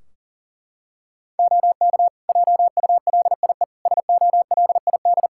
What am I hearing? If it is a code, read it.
Morse code, heard as a keyed tone, about 40 wpm, 700 Hz. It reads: OK JULIE SOLID
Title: Practice QSO - Exchange 3 - More Info - 40wpm